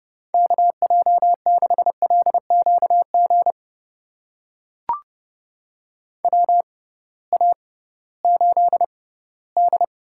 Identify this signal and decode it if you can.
Morse code, heard as a keyed tone, about 30 wpm, spaced as for 18 wpm, 700 Hz. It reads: KJ6LQG WA8D
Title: USA Call Signs - 6 Characters - 6x Character Spacing - 30wpm